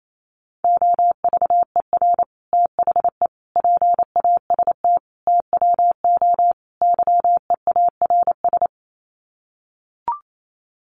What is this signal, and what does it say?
Morse code, heard as a keyed tone, about 28 wpm, 700 Hz. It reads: OVER THE PAST TWO YEARS
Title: Sets of 5 Words Encore 28wpm